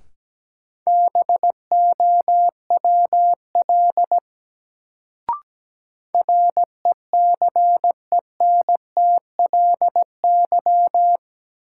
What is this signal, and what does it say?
Morse code, heard as a keyed tone, about 17 wpm, 700 Hz. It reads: BOWL RECENTLY
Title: Sentences from Top 1200 Words - New Word Drill Alt - 17wpm